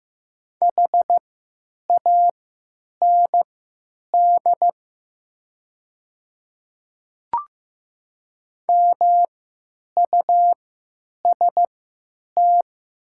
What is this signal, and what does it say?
Morse code, heard as a keyed tone, about 15 wpm, spaced as for 9 wpm, 700 Hz. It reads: HAND MUST
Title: 3x Character Spacing - Top 200 Words - 15wpm